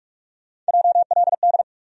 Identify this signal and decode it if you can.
Morse code, heard as a keyed tone, about 47 wpm, 700 Hz. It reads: JLD
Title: USA Call Sign Postfixes - 3 Letters 45wpm